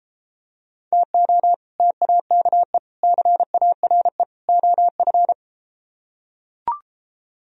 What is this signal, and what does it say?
Morse code, heard as a keyed tone, about 33 wpm, 700 Hz. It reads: TO TAKE CARE OF